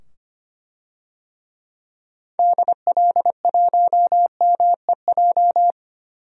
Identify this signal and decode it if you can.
Morse code, heard as a keyed tone, about 25 wpm, 700 Hz. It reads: DL1MEJ
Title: DXCC-100 Call Signs - 6 Characters - 25wpm